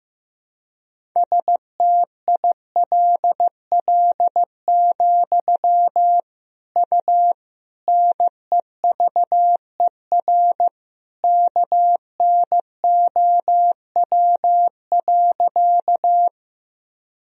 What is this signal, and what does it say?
Morse code, heard as a keyed tone, about 15 wpm, 700 Hz. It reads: STILL, U NEVER KNOW.
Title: Sentences from Top 300 Words and Abbreviations 15wpm